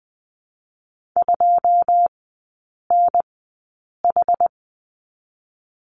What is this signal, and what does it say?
Morse code, heard as a keyed tone, about 20 wpm, spaced as for 8 wpm, 700 Hz. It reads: 2NH
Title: 081 Lesson 14e - Number 2 - Three Characters 20wpm